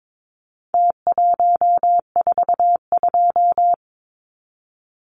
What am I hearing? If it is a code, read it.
Morse code, heard as a keyed tone, about 22 wpm, 700 Hz. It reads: T142
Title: Numbers - Contest Style - 4-Digit 22wpm